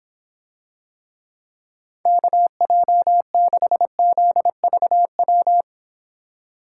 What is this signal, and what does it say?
Morse code, heard as a keyed tone, about 26 wpm, 700 Hz. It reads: KJ6ZVW